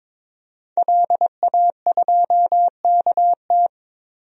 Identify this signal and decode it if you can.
Morse code, heard as a keyed tone, about 22 wpm, 700 Hz. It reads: LA2KT
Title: DXCC-100 Call Signs - 5 Characters - 22wpm